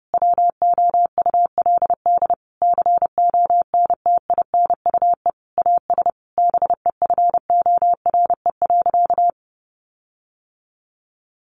Morse code, heard as a keyed tone, about 30 wpm, 700 Hz. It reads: WOULD CONTINUE AS BEFORE.
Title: Sentences from Top 1400 Words 30wpm